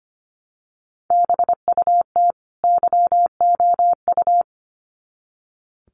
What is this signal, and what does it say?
Morse code, heard as a keyed tone, about 25 wpm, 700 Hz. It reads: BUT YOU
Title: Sets of 2 Words 25wpm